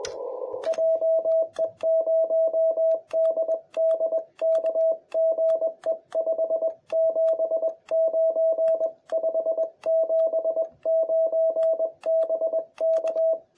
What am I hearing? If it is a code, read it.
Morse code, heard as a keyed tone, about 22 wpm, 650 Hz. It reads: JE0DDK GE 579 579 BK
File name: jh8cbh / 2021cw